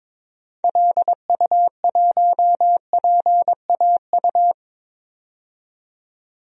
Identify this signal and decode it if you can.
Morse code, heard as a keyed tone, about 22 wpm, 700 Hz. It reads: LU1PAU
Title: DXCC-100 Call Signs - 6 Characters - 22wpm